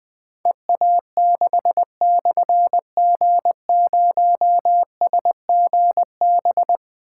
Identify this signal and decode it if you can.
Morse code, heard as a keyed tone, about 20 wpm, 700 Hz. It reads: EA6/G0SGB
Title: Difficult Call Signs 20wpm